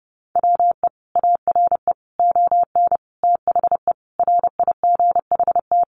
Morse code, heard as a keyed tone, about 30 wpm, 700 Hz. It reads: WE ARE ON THE RIGHT
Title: Sentences from Top 1100 Words 30wpm